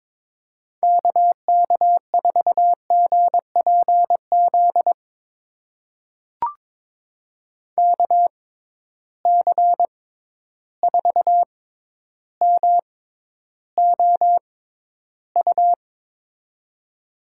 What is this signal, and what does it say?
Morse code, heard as a keyed tone, about 22 wpm, spaced as for 11 wpm, 700 Hz. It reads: KK4GPZ KC4MOU
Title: USA Call Signs - 6 Characters - 6x Character Spacing - 22wpm